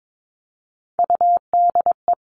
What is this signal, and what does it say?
Morse code, heard as a keyed tone, about 22 wpm, 700 Hz. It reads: UDE
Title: All Cut Numbers - 3-Digit 22wpm